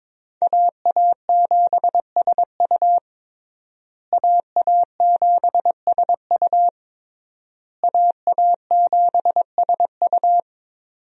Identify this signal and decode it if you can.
Morse code, heard as a keyed tone, about 22 wpm, 700 Hz. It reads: AA7SU AA7SU AA7SU
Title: USA Call Signs - 5 Characters - Repeated 3x - 22wpm